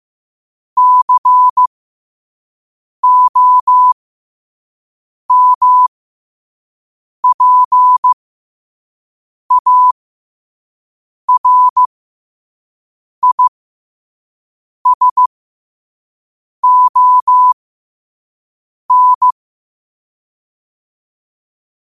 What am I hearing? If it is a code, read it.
Morse code, heard as a keyed tone, about 15 wpm, spaced as for 5 wpm, 1000 Hz. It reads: COMPARISON
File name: qotdfiles / 5wpm